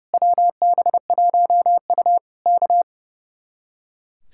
Morse code, heard as a keyed tone, about 28 wpm, 700 Hz. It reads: WB1U K
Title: Speed-Racing Common QSO Phrases Encore - 30wpm